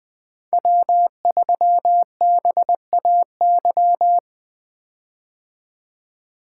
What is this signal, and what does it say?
Morse code, heard as a keyed tone, about 21 wpm, 700 Hz. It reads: W3BAY
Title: 2023 POTA Activator Call Signs - 20wpm